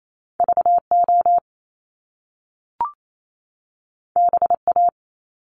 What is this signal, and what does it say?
Morse code, heard as a keyed tone, about 28 wpm, 700 Hz. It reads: V O B A
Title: Top Two Letter Combinations in Words 28wpm